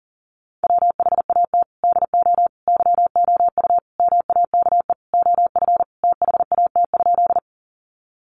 Morse code, heard as a keyed tone, about 40 wpm, 700 Hz. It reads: WHAT DO YOU MAKE OF THAT?
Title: Sentences from Top 100 Words 40wpm